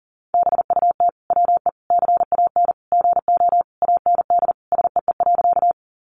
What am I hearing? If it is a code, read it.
Morse code, heard as a keyed tone, about 37 wpm, 700 Hz. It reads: BUT WE CAN GO AND SEE.
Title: Sentences from Top 100 Words 40wpm